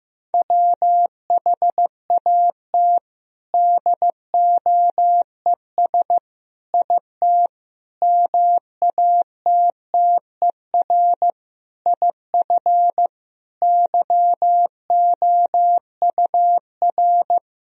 Morse code, heard as a keyed tone, about 16 wpm, 700 Hz. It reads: WHAT DOES IT MATTER IF YOUR
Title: Sentences from Top 1000 Words - Review Part 1 - 15wpm